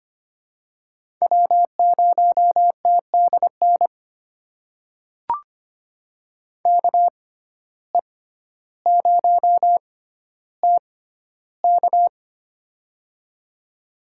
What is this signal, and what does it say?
Morse code, heard as a keyed tone, about 25 wpm, spaced as for 13 wpm, 700 Hz. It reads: W0TDN KE0TK
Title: USA Call Signs - 5 Characters - 6x Character Spacing - 25wpm